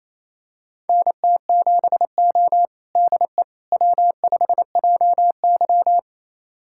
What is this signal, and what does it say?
Morse code, heard as a keyed tone, about 28 wpm, 700 Hz. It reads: NT7O DE W5JY
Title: USA Callsign DE Callsign - 4 Characters - 28wpm